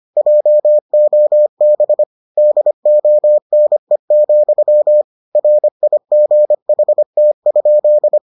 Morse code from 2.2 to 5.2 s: DONE,

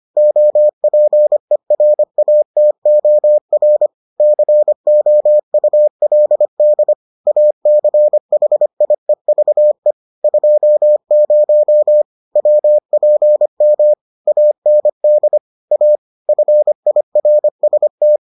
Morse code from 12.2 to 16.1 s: WPM AND A

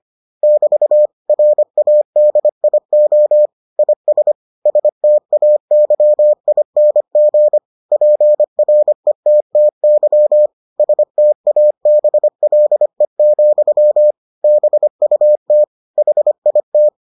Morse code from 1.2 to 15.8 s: RADIO IS STAYING PRETTY STABLE, BUT

